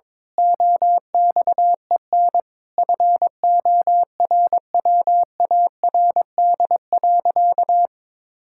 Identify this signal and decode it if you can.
Morse code, heard as a keyed tone, about 22 wpm, 700 Hz. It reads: OXEN FORWARD.
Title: Pangram Challenge v2 - 22wpm